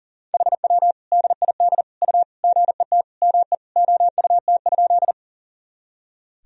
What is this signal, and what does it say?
Morse code, heard as a keyed tone, about 40 wpm, 700 Hz. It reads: HW DID U GET ME OUT?
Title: Sentences from Top 200 Words and Abbreviations 40wpm